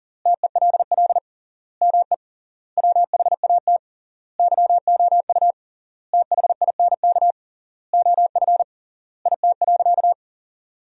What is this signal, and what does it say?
Morse code, heard as a keyed tone, about 40 wpm, 700 Hz. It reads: TELL ME WHAT YOU THINK OF IT.
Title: 3x Word-Spacing - Sentences from Top 200 Words - 40wpm